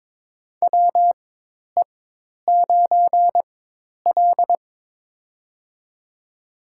Morse code, heard as a keyed tone, about 21 wpm, spaced as for 10 wpm, 700 Hz. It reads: WE9L
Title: USA Call Signs - 4x Character Spacing - 22wpm